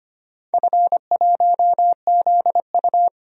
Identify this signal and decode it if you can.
Morse code, heard as a keyed tone, about 25 wpm, 700 Hz. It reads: F1ZU
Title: DXCC-100 Call Signs - 4 Characters - 25wpm